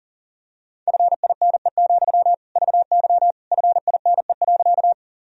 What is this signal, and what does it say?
Morse code, heard as a keyed tone, about 40 wpm, 700 Hz. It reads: FINE, VY FINE.
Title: Sentences from Top 500 Words and Abbreviations 40wpm